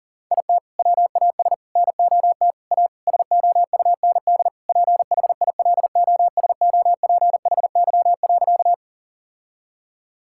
Morse code, heard as a keyed tone, about 37 wpm, 700 Hz. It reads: IT WAS NOT A SOUND PHILOSOPHY.
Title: Sentences from Top 1000 Words - Review Part 6 - 40wpm